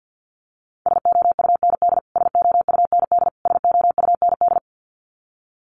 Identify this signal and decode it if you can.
Morse code, heard as a keyed tone, about 48 wpm, 700 Hz. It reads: SOUND SOUND SOUND
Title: Top 200 Words Repeated 50wpm